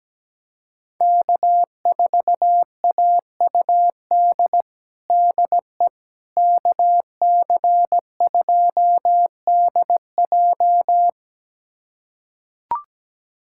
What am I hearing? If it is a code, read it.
Morse code, heard as a keyed tone, about 17 wpm, 700 Hz. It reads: K4AUD DE KC2DJ
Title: USA Callsign DE Callsign - 5 Characters - 17wpm